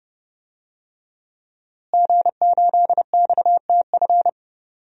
Morse code, heard as a keyed tone, about 30 wpm, 700 Hz. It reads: G8XTF
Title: DXCC-100 Call Signs - 5 Characters - Repeated 3x - 30wpm